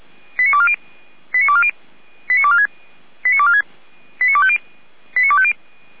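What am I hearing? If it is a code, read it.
Five-tone selective call (ZVEI), heard as quick runs of five tones, tones of about 70 ms. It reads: 89240 89240 89246 89246 89250 89250
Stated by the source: ZVEI five-tone selective call received by SDR.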